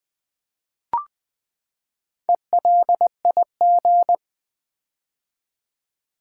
Morse code, heard as a keyed tone, about 20 wpm, 700 Hz. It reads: ELIG
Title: Build A Word - 4 Letters At A Time - 8-Letter Words - 20wpm